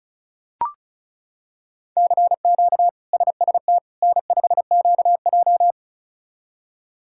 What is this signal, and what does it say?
Morse code, heard as a keyed tone, about 36 wpm, 700 Hz. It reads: CQ SST N5QJ
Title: K1USN SST QSO Practice - 35wpm